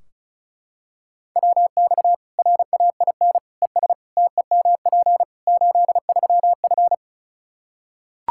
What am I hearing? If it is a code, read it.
Morse code, heard as a keyed tone, about 34 wpm, 700 Hz. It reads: WX RAIN ES TEMP 83F
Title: Practice QSO - Exchange 2 - Info - 35wpm